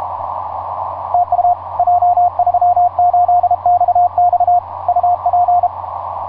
Morse code, heard as a keyed tone, about 32 wpm, 710 Hz, with radio noise. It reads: TU J38XX UP
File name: j38xx mw0cri etc 15m cw 18feb2010 1532